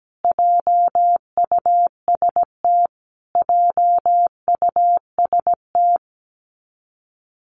Morse code, heard as a keyed tone, about 17 wpm, 700 Hz. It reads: JUST JUST
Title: Top 500 Words Repeated 17wpm